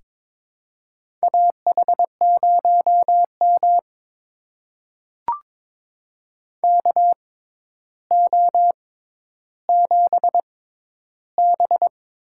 Morse code, heard as a keyed tone, about 22 wpm, spaced as for 11 wpm, 700 Hz. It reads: AH0M KO7B
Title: USA Call Signs - 4 Characters - 6x Character Spacing - 22wpm